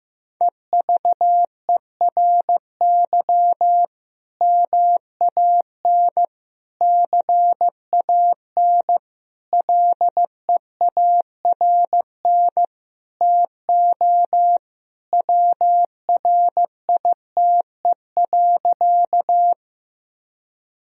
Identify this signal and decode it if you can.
Morse code, heard as a keyed tone, about 15 wpm, 700 Hz. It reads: EVERY MAN CAN LEARN TO WRITE.
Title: Sentences from Top 1000 Words - Review Part 1 - 15wpm